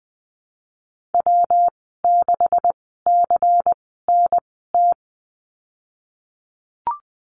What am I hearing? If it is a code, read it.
Morse code, heard as a keyed tone, about 20 wpm, spaced as for 15 wpm, 700 Hz. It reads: W6CNT
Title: 161 Lesson 26g - Letter g - Call Signs 20wpm